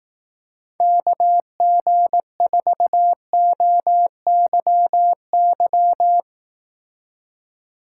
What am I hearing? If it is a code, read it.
Morse code, heard as a keyed tone, about 18 wpm, 700 Hz. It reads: KG4OYY
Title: Speed Racing - USA 2x3 Call Signs 15wpm